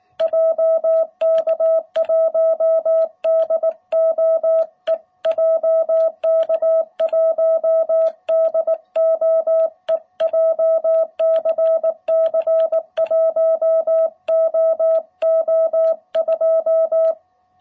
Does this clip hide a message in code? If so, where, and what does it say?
Yes: Morse code from 0.2 to 17.1 s, heard as a keyed tone, about 19 wpm, 660 Hz. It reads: JK1DOE JK1DOE JCC 1OO2